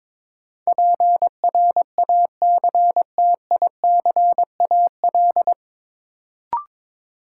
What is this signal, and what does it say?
Morse code, heard as a keyed tone, about 22 wpm, 700 Hz. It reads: PRACTICAL
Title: Sentences from Top 1400 Words - New Word Drill Alt - 22wpm